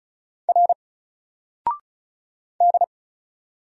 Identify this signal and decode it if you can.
Morse code, heard as a keyed tone, about 34 wpm, 700 Hz. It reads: R D
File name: Single-Letter-Number-35wpm